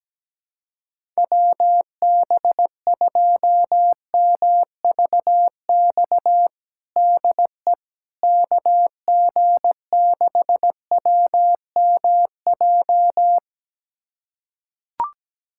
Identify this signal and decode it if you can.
Morse code, heard as a keyed tone, about 17 wpm, 700 Hz. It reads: WB2MVX DE KG6WMJ
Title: USA Callsign DE Callsign - 6 Characters - 17wpm